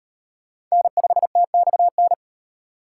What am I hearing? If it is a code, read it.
Morse code, heard as a keyed tone, about 36 wpm, 700 Hz. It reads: N5TXN